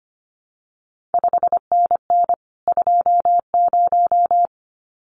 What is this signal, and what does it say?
Morse code, heard as a keyed tone, about 25 wpm, 700 Hz. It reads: 5NN 20
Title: CQ WW DX Contest Exchange - 25wpm